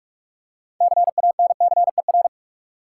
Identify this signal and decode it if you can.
Morse code, heard as a keyed tone, about 46 wpm, 700 Hz. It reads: CANCER
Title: Six Letter Words - 45wpm